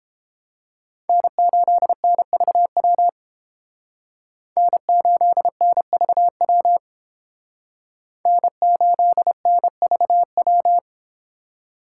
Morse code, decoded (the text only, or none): N8NVW N8NVW N8NVW